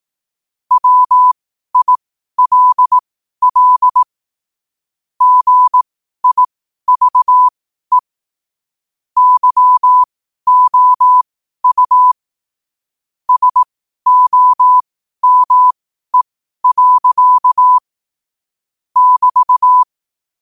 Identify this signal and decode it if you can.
Morse code, heard as a keyed tone, about 18 wpm, spaced as for 12 wpm, 1000 Hz. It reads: WILL GIVE YOU SOME. <BT>
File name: qotdfiles / 12wpm